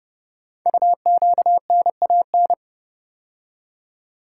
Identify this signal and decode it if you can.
Morse code, heard as a keyed tone, about 30 wpm, 700 Hz. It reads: UQNAN
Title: Code Groups - 5 Letters - 30wpm